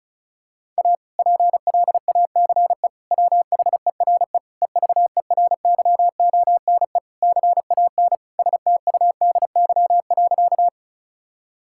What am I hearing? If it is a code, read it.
Morse code, heard as a keyed tone, about 36 wpm, 700 Hz. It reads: A PLACE WHERE EVERYONE CAN STUDY.
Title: Sentences from Top 1000 Words - Review Part 5 - 35wpm